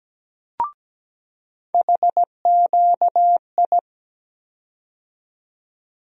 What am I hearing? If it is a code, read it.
Morse code, heard as a keyed tone, about 17 wpm, 700 Hz. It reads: HQI
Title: USA Call Sign Postfixes - 3 Letters 17wpm